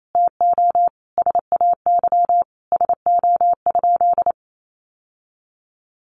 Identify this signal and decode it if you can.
Morse code, heard as a keyed tone, about 28 wpm, 700 Hz. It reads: TO SAY SO?
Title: Sentences from Top 1200 Words 28wpm